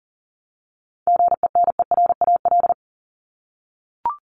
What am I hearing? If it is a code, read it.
Morse code, heard as a keyed tone, about 40 wpm, 700 Hz. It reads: GENERAL
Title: LY Suffix - Top 100 Words with Base Words 40wpm